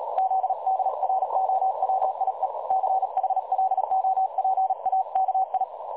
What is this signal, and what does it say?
Morse code, heard as a keyed tone, about 38 wpm, 760 Hz, with noise. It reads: ZDROWIE CHLOPAKI
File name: dlazaawansowanych / nagraniazpasma / websdr recording 3565.0kHz